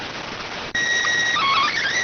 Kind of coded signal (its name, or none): SSTV (slow-scan television)